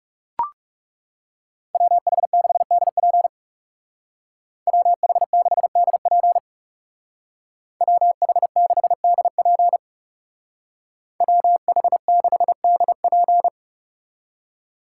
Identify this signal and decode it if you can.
Morse code, heard as a keyed tone, about 40 wpm, 700 Hz. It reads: WH6DP WH6DP WH6DP WH6DP